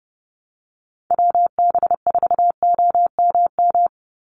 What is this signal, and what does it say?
Morse code, heard as a keyed tone, about 30 wpm, 700 Hz. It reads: WB4OMM
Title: Call Signs Repeated 2x - 30wpm